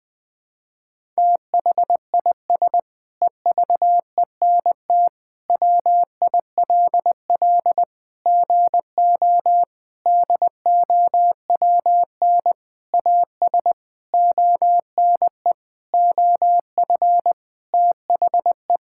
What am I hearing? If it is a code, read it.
Morse code, heard as a keyed tone, about 20 wpm, 700 Hz. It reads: THIS EVENT WILL GO DOWN AS ONE OF THE
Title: Sentences with 7-Word Phrases - 20wpm